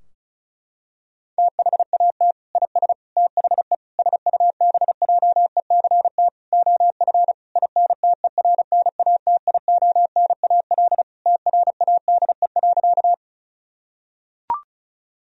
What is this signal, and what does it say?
Morse code, heard as a keyed tone, about 34 wpm, 700 Hz. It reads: THAT IS THE SUBJECT OF INTERNATIONAL TRADE.
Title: Sentences from Top 1000 Words - Review Part 4 - 35wpm